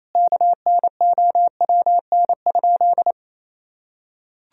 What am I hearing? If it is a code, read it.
Morse code, heard as a keyed tone, about 28 wpm, 700 Hz. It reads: KNOWN?
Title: Sentences from Top 1000 Words - Review Part 3 - 28wpm